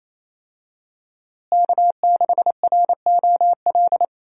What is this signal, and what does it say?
Morse code, heard as a keyed tone, about 28 wpm, 700 Hz. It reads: K6ROL
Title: USA Call Signs - 5 Characters - Repeated 2x - 28wpm